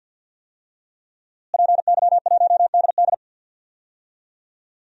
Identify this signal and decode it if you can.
Morse code, heard as a keyed tone, about 51 wpm, 700 Hz. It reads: PY1DD
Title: DXCC-100 Call Signs - 5 Characters - Repeated 3x - 50wpm